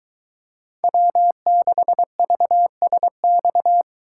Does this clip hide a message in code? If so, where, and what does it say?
Yes: Morse code from 0.8 to 3.8 s, heard as a keyed tone, about 22 wpm, 700 Hz. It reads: W6VSX